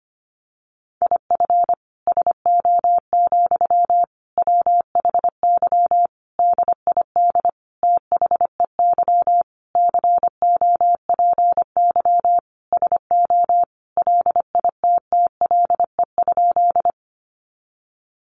Morse code, heard as a keyed tone, about 25 wpm, 700 Hz. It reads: IF SO, WHY DID THEY COPY SO LITTLE?
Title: Sentences from Top 1400 Words 25wpm